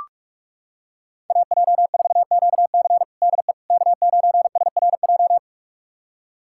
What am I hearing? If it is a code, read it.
Morse code, heard as a keyed tone, about 43 wpm, 700 Hz. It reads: AJ4QC DE K9SRJ
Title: USA Callsign DE Callsign - 5 Characters - 45wpm